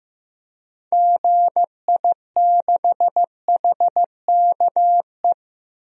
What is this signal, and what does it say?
Morse code, heard as a keyed tone, about 14 wpm, 700 Hz. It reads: GI6HKE
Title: DXCC-100 Call Signs - 6 Characters - 15wpm